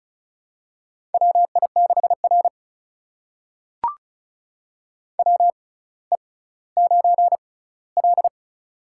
Morse code, heard as a keyed tone, about 36 wpm, spaced as for 18 wpm, 700 Hz. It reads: WI6R WE9L
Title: USA Call Signs - 4 Characters - 6x Character Spacing - 35wpm